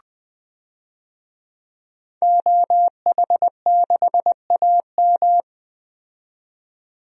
Morse code, heard as a keyed tone, about 20 wpm, 700 Hz. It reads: OH6AM
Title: DXCC-100 Call Signs - 5 Characters - 20wpm